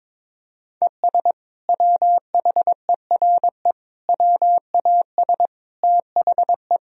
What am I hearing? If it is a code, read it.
Morse code, heard as a keyed tone, about 22 wpm, 700 Hz. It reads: ES WHERE WAS THE
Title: Sentences from Top 200 Words and Abbreviations 22wpm